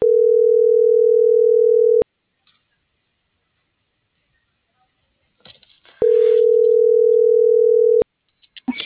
Ringback tone (North America), 440 + 480 Hz.